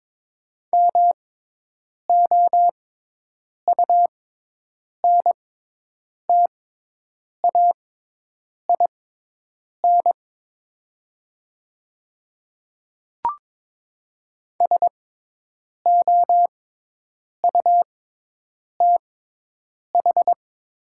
Morse code, heard as a keyed tone, about 22 wpm, spaced as for 8 wpm, 700 Hz. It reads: MOUNTAIN SOUTH
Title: 6x Character Spacing - Top 500 Words - 22wpm